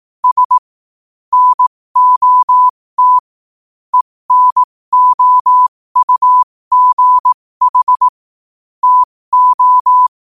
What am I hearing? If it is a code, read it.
Morse code, heard as a keyed tone, about 18 wpm, spaced as for 15 wpm, 1000 Hz. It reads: S NOT ENOUGH TO
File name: qotdfiles / 15wpm